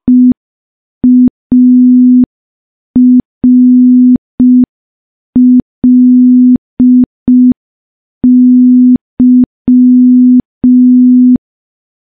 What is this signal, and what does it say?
Morse code, heard as a keyed tone, about 5 wpm, 250 Hz. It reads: EARLY
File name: morse / morse-training-05WPM